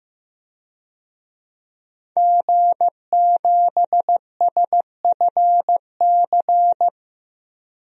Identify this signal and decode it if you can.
Morse code, heard as a keyed tone, about 15 wpm, 700 Hz. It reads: G7SFC
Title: DXCC-100 Call Signs - 5 Characters - 15wpm